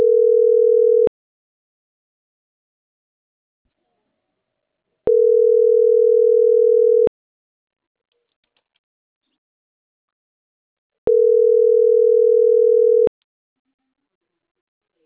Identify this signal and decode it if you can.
Ringback tone (North America), 440 + 480 Hz.